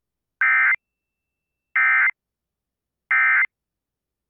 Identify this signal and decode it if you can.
Emergency Alert System (SAME) tones, heard as harsh buzzing bursts. It is heard from 0.4 to 3.4 s.